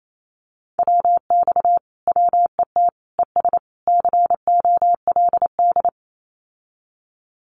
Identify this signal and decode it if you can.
Morse code, heard as a keyed tone, about 28 wpm, 700 Hz. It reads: WX WET ES COLD
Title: Common QSO Phrases Encore 28wpm